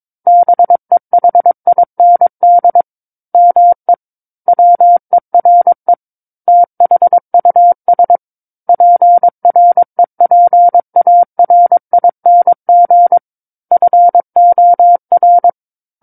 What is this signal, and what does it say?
Morse code, heard as a keyed tone, about 22 wpm, 700 Hz. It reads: BEHIND ME WERE THUS PREPARING FOR
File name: Book One / 13-Chapter 13 - 22 WPM